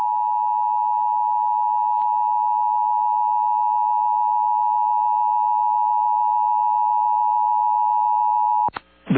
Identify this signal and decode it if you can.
Emergency Alert System (SAME) tones (the 853 and 960 Hz attention tone). It is heard from 0.5 to 8.0 s.
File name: eas / KUSO / KUSO